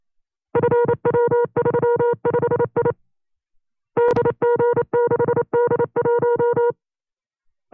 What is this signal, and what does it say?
Morse code, heard as a keyed tone, about 28 wpm, 480 Hz. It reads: FW35I DG6D1
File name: CW-practice/25WPM